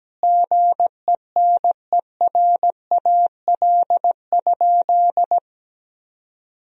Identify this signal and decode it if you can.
Morse code, heard as a keyed tone, about 17 wpm, 700 Hz. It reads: GENERAL?
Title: Sentences from Top 1000 Words - Review Part 3 - 17wpm